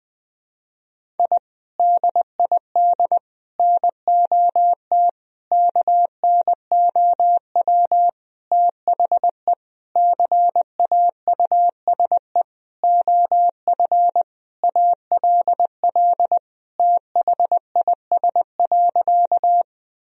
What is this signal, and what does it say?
Morse code, heard as a keyed tone, about 20 wpm, 700 Hz. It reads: I DID NOT KNOW THE CAUSE OF ALL THIS.
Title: Sentences from Top 1000 Words - Review Part 1 - 20wpm